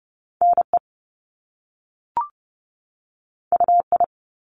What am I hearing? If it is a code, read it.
Morse code, heard as a keyed tone, about 30 wpm, 700 Hz. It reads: NE UI
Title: Top Two Letter Combinations in Words 30wpm